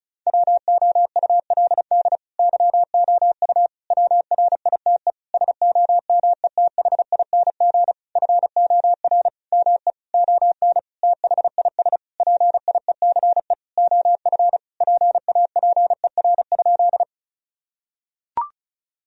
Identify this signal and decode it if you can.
Morse code, heard as a keyed tone, about 36 wpm, 700 Hz. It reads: WOULD YOU WRITE SOMETHING FOR ME ON THIS PIECE OF PAPER?
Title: Sentences from Top 1000 Words - Review Part 2 - 35wpm